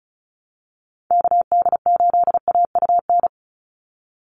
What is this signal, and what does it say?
Morse code, heard as a keyed tone, about 34 wpm, 700 Hz. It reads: KD8AUN
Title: USA Call Signs 35wpm